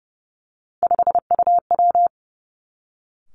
Morse code, heard as a keyed tone, about 30 wpm, 700 Hz. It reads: 5UW
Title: Three Challenging Letter-Number Combinations 30wpm